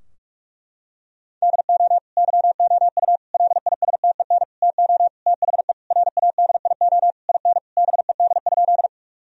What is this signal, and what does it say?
Morse code, heard as a keyed tone, about 46 wpm, 700 Hz. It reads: DO YOU LISTEN TO THE RADIO IN BED?
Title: Sentences from Top 1000 Words - Review Part 3 - 45wpm